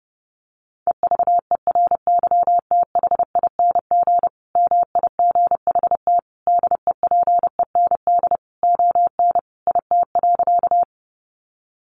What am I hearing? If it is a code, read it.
Morse code, heard as a keyed tone, about 30 wpm, 700 Hz. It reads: EVERYTHING MIGHT DEPEND ON IT.
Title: Sentences from Top 1300 Words 30wpm